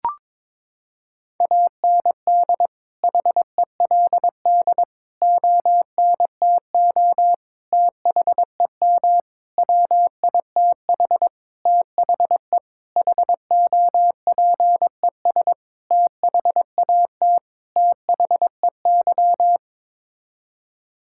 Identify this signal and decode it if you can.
Morse code, heard as a keyed tone, about 22 wpm, 700 Hz. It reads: AND HELD ONTO THEM WITH THE HOPES THAT THEY
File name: Sets of 9 Words - 22wpm